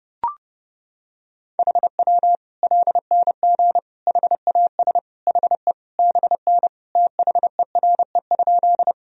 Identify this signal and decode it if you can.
Morse code, heard as a keyed tone, about 30 wpm, 700 Hz. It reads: HW LNG HAS HE BN THERE?
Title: Sentences from Top 100 Words and Abbreviations 30wpm